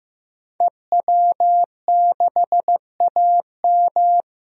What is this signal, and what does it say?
Morse code, heard as a keyed tone, about 15 wpm, 700 Hz. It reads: EW6AM
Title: DXCC-100 Call Signs - 5 Characters - 15wpm